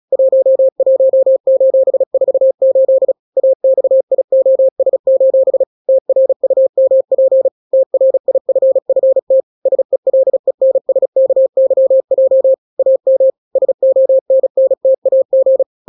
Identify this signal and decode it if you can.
Morse code, heard as a keyed tone, about 36 wpm, 530 Hz. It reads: 11:48 AXIOS: TRUMP TRIFFT SELENSKYJ AM SONNTAG
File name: daily / 35wpm